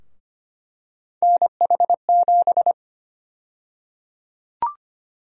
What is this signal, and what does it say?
Morse code, heard as a keyed tone, about 25 wpm, 700 Hz. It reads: NH7
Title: USA Call Sign Prefixes - Repeated 2x - 25wpm